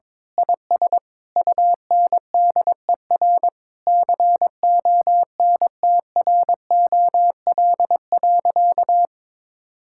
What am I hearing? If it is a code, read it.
Morse code, heard as a keyed tone, about 22 wpm, 700 Hz. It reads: IS UNDER CONTROL.
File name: Sentences with 9-Word Phrases - 22wpm